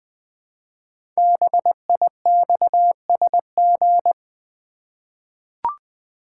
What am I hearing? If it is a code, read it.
Morse code, heard as a keyed tone, about 21 wpm, 700 Hz. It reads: BIXSG
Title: Code Groups - 5 Letters - 20wpm